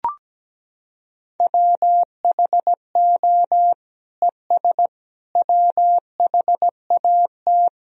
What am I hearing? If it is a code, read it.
Morse code, heard as a keyed tone, about 17 wpm, 700 Hz. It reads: WHO ES WHAT
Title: Sentences from Top 100 Words and Abbreviations 17wpm